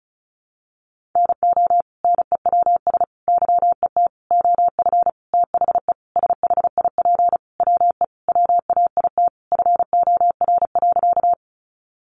Morse code, heard as a keyed tone, about 34 wpm, 700 Hz. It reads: NO NEWS YET OF THE SHIP WE WAIT FOR.
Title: Sentences from Top 800 Words 35wpm